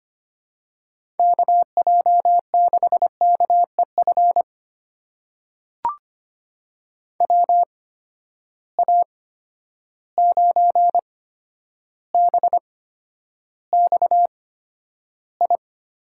Morse code, heard as a keyed tone, about 25 wpm, spaced as for 11 wpm, 700 Hz. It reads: KJ6KEF WA9BXI
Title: USA Call Signs - 6 Characters - 8x Character Spacing - 25wpm